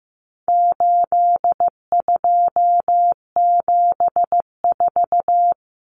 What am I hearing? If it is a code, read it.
Morse code, heard as a keyed tone, about 15 wpm, 700 Hz. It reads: 8274
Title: Numbers - 4-Digit 15wpm